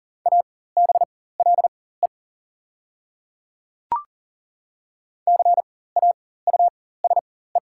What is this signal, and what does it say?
Morse code, heard as a keyed tone, about 40 wpm, spaced as for 19 wpm, 690 Hz. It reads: ABLE CAUSE
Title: 4x Character Spacing - Top 500 Words - 40wpm